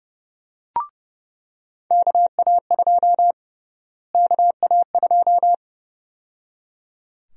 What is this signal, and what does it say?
Morse code, heard as a keyed tone, about 30 wpm, 700 Hz. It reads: KA2 KA2
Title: USA Call Sign Prefixes - Repeated 2x - 30wpm